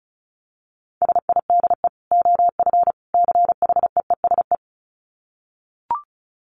Morse code, heard as a keyed tone, about 34 wpm, 700 Hz. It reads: SIDE OF CHEESE
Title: Dit Challenge 35wpm